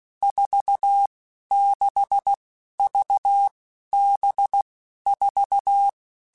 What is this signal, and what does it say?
Morse code, heard as a keyed tone, about 16 wpm, 800 Hz. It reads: 4 6 V B 4
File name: Basic Letter Groups By7Groups / VB46 / VB46@15